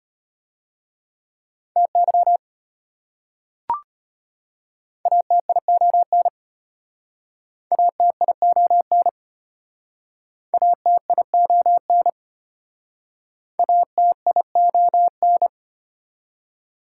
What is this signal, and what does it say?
Morse code, heard as a keyed tone, about 32 wpm, 700 Hz. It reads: TY ATION ATION ATION ATION